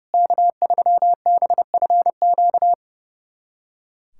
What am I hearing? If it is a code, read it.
Morse code, heard as a keyed tone, about 30 wpm, 700 Hz. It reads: K3BFQ
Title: USA Callsign DE Callsign - 5 Characters - 30wpm